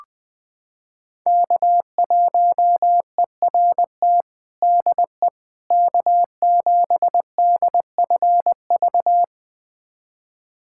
Morse code, heard as a keyed tone, about 19 wpm, 700 Hz. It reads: K1ERT DE K7DFV
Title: USA Callsign DE Callsign - 5 Characters - 20wpm